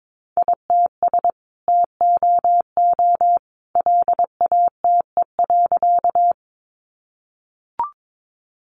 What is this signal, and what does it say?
Morse code, heard as a keyed tone, about 22 wpm, 700 Hz. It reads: ITS TOO LATE.
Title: Sentences from Top 600 Words 22wpm